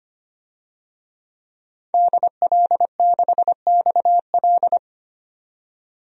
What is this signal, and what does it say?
Morse code, heard as a keyed tone, about 25 wpm, 700 Hz. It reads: DL6XL
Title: DXCC-100 Call Signs - 5 Characters - 25wpm